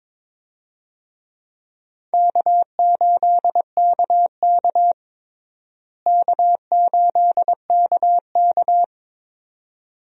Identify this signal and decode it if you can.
Morse code, heard as a keyed tone, about 21 wpm, 700 Hz. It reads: K8KK K8KK
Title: USA Call Signs - 4 Characters - Repeated 2x - 22wpm